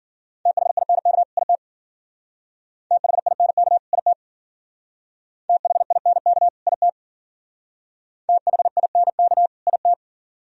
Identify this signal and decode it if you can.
Morse code, heard as a keyed tone, about 48 wpm, 700 Hz. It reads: THINK IT THINK IT THINK IT THINK IT
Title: Speed-Racing - Sets of 2 Words 40wpm